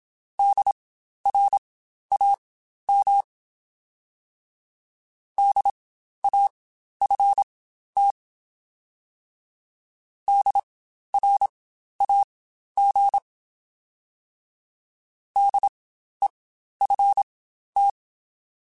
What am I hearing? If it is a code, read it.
Morse code, heard as a keyed tone, about 27 wpm, spaced as for 12 wpm, 800 Hz. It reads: DRAM DAFT DRAG DEFT
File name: Spaced Alphas / Spaced Alphas mp3s / Ds sp @25